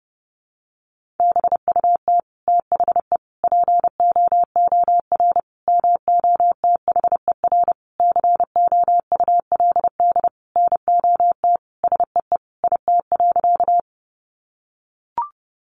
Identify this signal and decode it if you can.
Morse code, heard as a keyed tone, about 30 wpm, 700 Hz. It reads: BUT THE POOR MOTHER COULD NOT SEE IT.
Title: Sentences from Top 1000 Words 30wpm